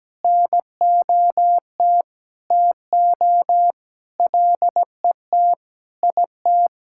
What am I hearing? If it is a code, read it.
Morse code, heard as a keyed tone, about 17 wpm, 700 Hz. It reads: NOT TO LET IT
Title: Sentences from Top 1000 Words - Review Part 6 - 17wpm